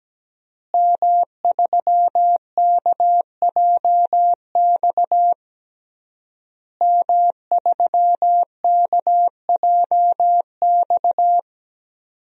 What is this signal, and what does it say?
Morse code, heard as a keyed tone, about 17 wpm, 700 Hz. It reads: M3KJX M3KJX
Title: DXCC-100 Call Signs - 5 Characters - Repeated 3x - 17wpm